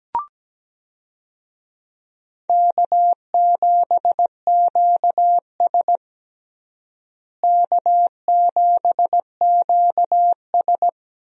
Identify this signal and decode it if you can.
Morse code, heard as a keyed tone, about 17 wpm, 700 Hz. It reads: K7QS K7QS
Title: USA Call Signs - 4 Characters - Repeated 3x - 17wpm